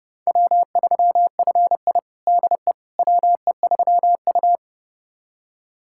Morse code, heard as a keyed tone, about 30 wpm, 700 Hz. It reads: W3FI DE WE3U
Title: USA Callsign DE Callsign - 4 Characters - 30wpm